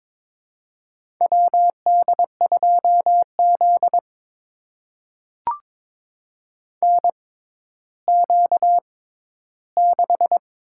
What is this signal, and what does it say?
Morse code, heard as a keyed tone, about 22 wpm, spaced as for 13 wpm, 700 Hz. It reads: WD2Z NQ6
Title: USA Call Signs - 4 Characters - 6x Character Spacing - 22wpm